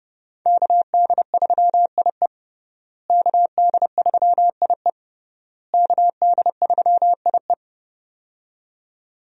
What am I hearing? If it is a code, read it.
Morse code, heard as a keyed tone, about 30 wpm, 700 Hz. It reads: KD3IE KD3IE KD3IE
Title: USA Call Signs - 5 Characters - Repeated 3x - 30wpm